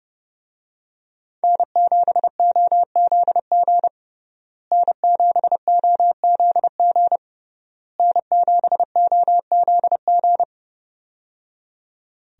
Morse code, heard as a keyed tone, about 30 wpm, 700 Hz. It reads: N7OZG N7OZG N7OZG
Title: USA Call Signs - 5 Characters - Repeated 3x - 30wpm